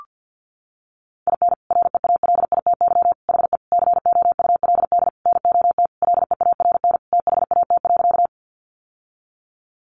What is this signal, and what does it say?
Morse code, heard as a keyed tone, about 50 wpm, 700 Hz. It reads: IN REALITY HE COULD NOT LEARN THAT.
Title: Sentences from Top 1200 Words 50wpm